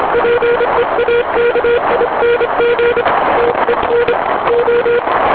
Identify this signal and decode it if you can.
Morse code, heard as a keyed tone, about 27 wpm, 480 Hz, with radio noise. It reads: PEAKING TEN O
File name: j7 vp2m 2002 / Voice / J75PA 160 cw